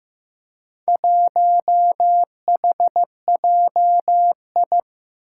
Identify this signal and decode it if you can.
Morse code, heard as a keyed tone, about 15 wpm, 700 Hz. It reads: 1HJI
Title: Four Challenging Letter-Number Combinations 15wpm